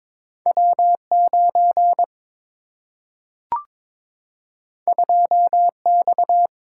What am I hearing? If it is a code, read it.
Morse code, heard as a keyed tone, about 22 wpm, 700 Hz. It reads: W 9 2 X
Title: Single-Letter Single-Number 22wpm